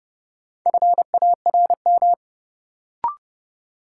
Morse code, heard as a keyed tone, about 31 wpm, 700 Hz. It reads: FARM
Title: Sentences from Top 1000 Words - Review Part 2 - New Word Drill - 30wpm